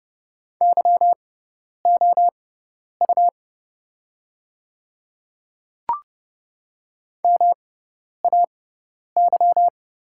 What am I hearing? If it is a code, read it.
Morse code, heard as a keyed tone, about 30 wpm, spaced as for 10 wpm, 700 Hz. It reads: YOU MAY
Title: 6x Character Spacing - Top 500 Words - 30wpm